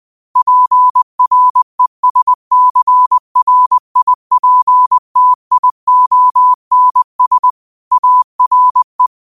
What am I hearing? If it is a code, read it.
Morse code, heard as a keyed tone, about 20 wpm, 1000 Hz. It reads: PRESCRIPTIONS ARE
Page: New quotes every day in morse code at 20 Words per minute.